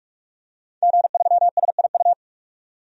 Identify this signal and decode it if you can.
Morse code, heard as a keyed tone, about 44 wpm, 690 Hz. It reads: G3SIU
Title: DXCC-100 Call Signs - 5 Characters - 45wpm